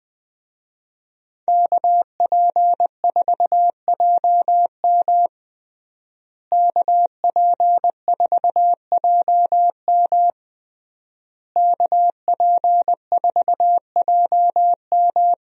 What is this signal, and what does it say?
Morse code, heard as a keyed tone, about 20 wpm, 700 Hz. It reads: KP4JM KP4JM KP4JM
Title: DXCC-100 Call Signs - 5 Characters - Repeated 3x - 20wpm